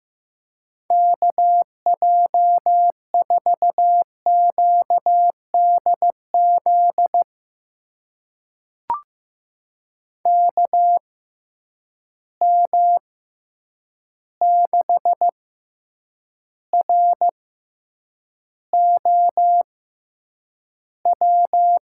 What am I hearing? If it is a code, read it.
Morse code, heard as a keyed tone, about 15 wpm, spaced as for 8 wpm, 700 Hz. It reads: KJ4QDZ KM6ROW
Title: USA Call Signs - 6 Characters - 6x Character Spacing - 15wpm